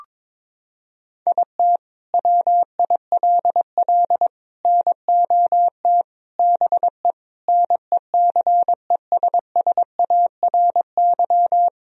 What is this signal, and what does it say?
Morse code, heard as a keyed tone, about 22 wpm, 700 Hz. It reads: IT WILL NOT BE NECESSARY
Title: Sentences from Top 1000 Words - Review Part 1 - 22wpm